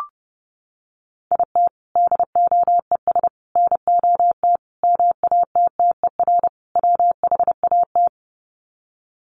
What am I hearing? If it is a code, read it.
Morse code, heard as a keyed tone, about 30 wpm, 700 Hz. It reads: IT DOES NOT MATTER WHAT
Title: Sets of 5 Words Encore 30wpm